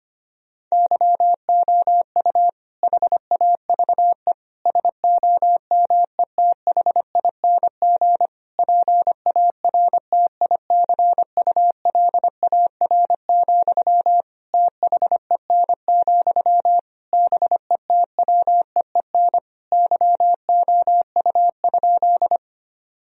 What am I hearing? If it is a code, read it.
Morse code, heard as a keyed tone, about 25 wpm, 700 Hz. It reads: YOU HAVE SOMETHING PARTICULAR, THEN, BETWEEN YOU?
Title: Sentences from Top 1000 Words - Review Part 4 - 25wpm